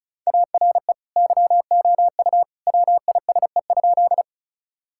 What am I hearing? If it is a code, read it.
Morse code, heard as a keyed tone, about 36 wpm, 700 Hz. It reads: ARE YOU WISE?
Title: Sentences from Top 1600 Words 35wpm